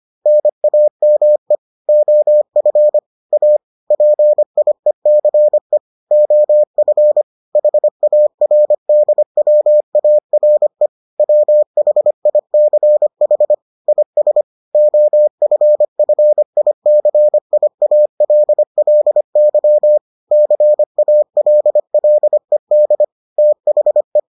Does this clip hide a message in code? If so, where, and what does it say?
Yes: Morse code from 0.3 to 24.2 s, heard as a keyed tone, about 25 wpm, 600 Hz. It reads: NAME OF A PIECE OF HARDWARE WHICH IS OFFICIALLY CALLED THE